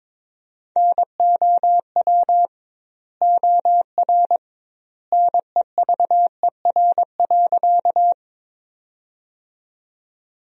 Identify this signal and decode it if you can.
Morse code, heard as a keyed tone, about 22 wpm, 700 Hz. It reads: NOW OR NEVER.
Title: 2x Word-Spacing - Sentences from Top 300 Words - 22wpm